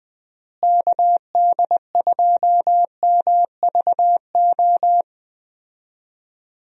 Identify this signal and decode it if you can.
Morse code, heard as a keyed tone, about 20 wpm, 700 Hz. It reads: KD2MVO
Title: USA Call Signs - 6 Characters - Repeated 2x - 20wpm